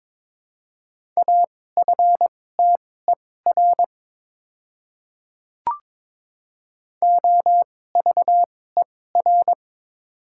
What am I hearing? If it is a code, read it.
Morse code, heard as a keyed tone, about 22 wpm, spaced as for 16 wpm, 700 Hz. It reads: AFTER OVER
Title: 2x Character Spacing - Top 200 Words - 22wpm